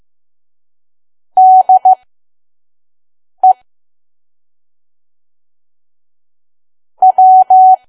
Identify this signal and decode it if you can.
Morse code, heard as a keyed tone, about 15 wpm, spaced as for 5 wpm, 750 Hz. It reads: DE W
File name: Archive / 5 WPM files / PM